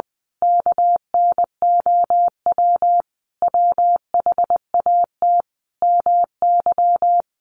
Morse code, heard as a keyed tone, about 19 wpm, 700 Hz. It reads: KNOW WHAT MY
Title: Sentences from Top 1200 Words 20wpm